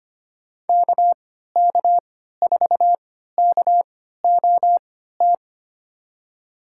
Morse code, heard as a keyed tone, about 25 wpm, spaced as for 14 wpm, 700 Hz. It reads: KK4KOT
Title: USA Call Signs - 6 Characters - 3x Character Spacing - 25wpm